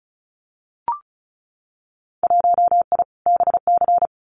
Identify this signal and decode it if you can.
Morse code, heard as a keyed tone, about 34 wpm, 700 Hz. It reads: 1I BC
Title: Winter Field Day Exchanges Encore 35wpm